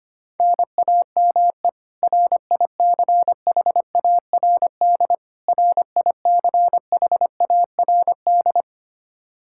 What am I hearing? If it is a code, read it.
Morse code, heard as a keyed tone, about 25 wpm, 700 Hz. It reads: NAME RICHARD RICHARD
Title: Practice QSO - Exchange 1 - Beginning - 25wpm